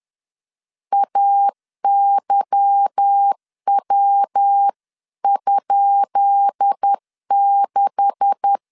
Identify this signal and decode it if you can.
Morse code, heard as a keyed tone, about 10 wpm, 800 Hz. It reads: AYW?6
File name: LEZIONI / 10 SIGNS / 10C DOT 10wpm 2940